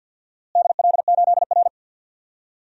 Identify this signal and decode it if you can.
Morse code, heard as a keyed tone, about 51 wpm, 700 Hz. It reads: DL8R
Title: DXCC-100 Call Signs - 4 Characters - Repeated 2x - 50wpm